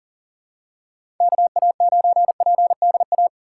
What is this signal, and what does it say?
Morse code, heard as a keyed tone, about 40 wpm, 700 Hz. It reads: KA9PDA
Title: USA Call Signs - 6 Characters - 3x Character Spacing - 40wpm